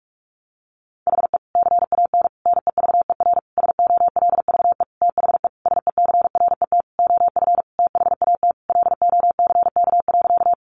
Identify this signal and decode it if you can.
Morse code, heard as a keyed tone, about 44 wpm, 700 Hz. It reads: HE CAN NEVER SOLVE THE SECRET OF THAT LOCK.
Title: Sentences from Top 1300 Words 45wpm